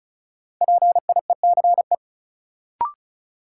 Morse code, heard as a keyed tone, about 36 wpm, 700 Hz. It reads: PIECE
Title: Sentences from Top 400 Words - New Word Drill - 35wpm